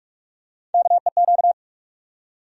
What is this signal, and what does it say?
Morse code, heard as a keyed tone, about 48 wpm, 700 Hz. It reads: KEQ
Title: USA Call Sign Postfixes - 3 Letters 45wpm